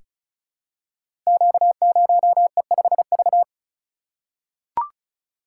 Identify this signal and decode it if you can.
Morse code, heard as a keyed tone, about 36 wpm, 700 Hz. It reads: Q0E5V
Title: Code Groups - 5 Letter:Numbers - 35wpm